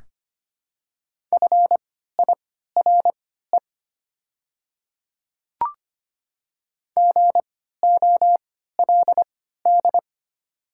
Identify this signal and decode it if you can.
Morse code, heard as a keyed tone, about 25 wpm, spaced as for 14 wpm, 690 Hz. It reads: FIRE GOLD
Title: 3x Character Spacing - Top 500 Words - 25wpm